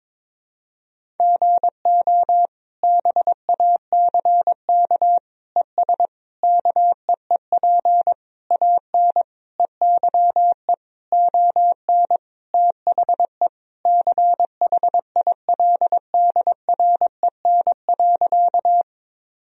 Morse code, heard as a keyed tone, about 21 wpm, 700 Hz. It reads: GO BACK ES KEEP AN EYE ON THE CHILDREN.
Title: Sentences from Top 300 Words and Abbreviations 22wpm